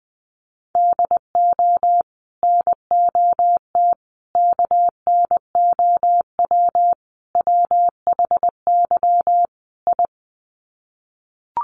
Morse code, heard as a keyed tone, about 20 wpm, 700 Hz. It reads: DO NOT KNOW WHY I
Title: Sets of 5 Words Encore 20wpm